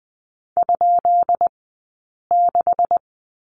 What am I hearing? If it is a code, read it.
Morse code, heard as a keyed tone, about 20 wpm, 700 Hz. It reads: ? 6
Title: 165 Lesson 27d - Number 7 - Two Characters 20wpm